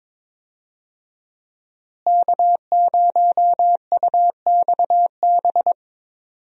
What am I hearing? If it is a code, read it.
Morse code, heard as a keyed tone, about 21 wpm, 700 Hz. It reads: K0UXB
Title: USA Call Signs - 5 Characters - Repeated 2x - 22wpm